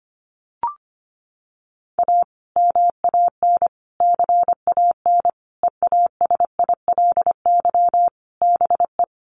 Morse code, heard as a keyed tone, about 25 wpm, 700 Hz. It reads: A MAN CAN EASILY BE
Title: Sentences from Top 900 Words 25wpm